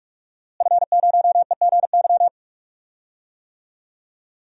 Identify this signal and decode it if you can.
Morse code, heard as a keyed tone, about 45 wpm, 700 Hz. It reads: F0EGY
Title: DXCC-100 Call Signs - 5 Characters - Repeated 2x - 45wpm